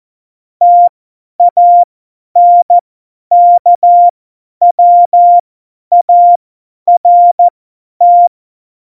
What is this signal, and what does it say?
Morse code, heard as a keyed tone, about 14 wpm, spaced as for 10 wpm, 700 Hz. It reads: TANKWART
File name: Morsekurs / Witze / Witz 12. Februar 2019, 50 F70